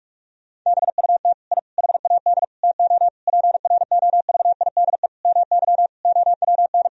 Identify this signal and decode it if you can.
Morse code, heard as a keyed tone, about 46 wpm, 700 Hz. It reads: BUT I HAD TO PROVIDE MY OWN
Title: Sentences from Top 1000 Words - Review Part 3 - 45wpm